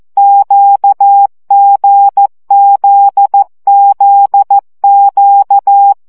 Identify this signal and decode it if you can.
Morse code, heard as a keyed tone, about 14 wpm, 800 Hz. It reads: QGZZQ
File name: CW gyakorlas g z q 12 baud